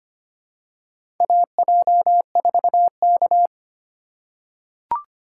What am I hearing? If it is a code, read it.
Morse code, heard as a keyed tone, about 25 wpm, 700 Hz. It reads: AJ4K
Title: USA Call Signs - 4 Characters - 2x Character Spacing - 25wpm